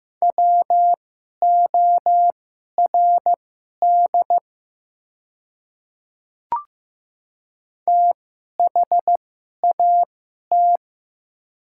Morse code, heard as a keyed tone, about 15 wpm, spaced as for 11 wpm, 700 Hz. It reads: WORD THAT
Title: 2x Character Spacing - Top 200 Words - 15wpm